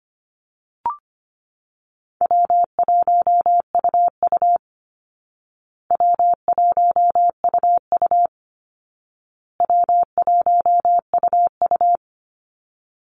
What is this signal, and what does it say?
Morse code, heard as a keyed tone, about 25 wpm, 700 Hz. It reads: W1UU W1UU W1UU
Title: Call Signs Repeated 2x - 25wpm